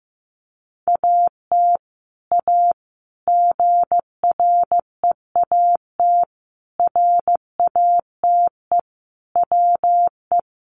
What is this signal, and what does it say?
Morse code, heard as a keyed tone, about 14 wpm, 700 Hz. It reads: AT A GREAT RATE WE
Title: Sentences from Top 1200 Words 15wpm